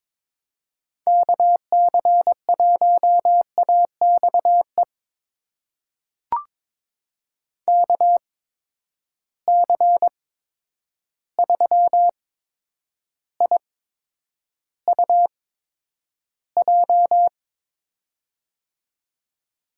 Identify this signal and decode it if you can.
Morse code, heard as a keyed tone, about 21 wpm, spaced as for 9 wpm, 700 Hz. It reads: KC1AXE KC3IUJ
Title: USA Call Signs - 6 Characters - 8x Character Spacing - 22wpm